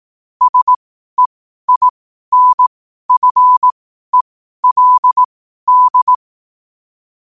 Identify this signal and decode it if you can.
Morse code, heard as a keyed tone, about 18 wpm, spaced as for 12 wpm, 1000 Hz. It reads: SEINFELD
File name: qotdfiles / 12wpm